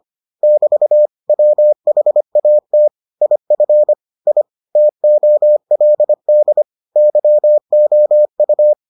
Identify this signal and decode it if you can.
Morse code, heard as a keyed tone, about 25 wpm, 600 Hz. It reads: <BT> WHAT IF I TOLD YOU